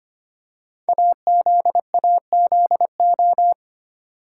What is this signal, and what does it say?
Morse code, heard as a keyed tone, about 25 wpm, 700 Hz. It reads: AZAZO
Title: Code Groups - 5 Letters - 25wpm